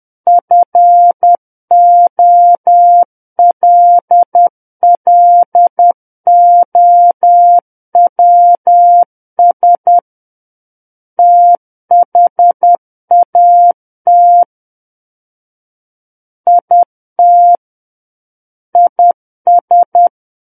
Morse code, heard as a keyed tone, about 10 wpm, 700 Hz. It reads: FOLLOWS THAT IT IS
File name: Book One / 01-Chapter 1 - 10 WPM